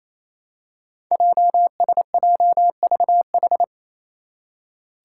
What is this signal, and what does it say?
Morse code, heard as a keyed tone, about 28 wpm, 700 Hz. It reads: JSJVH
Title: Five Challenging Letter-Number Combinations 28wpm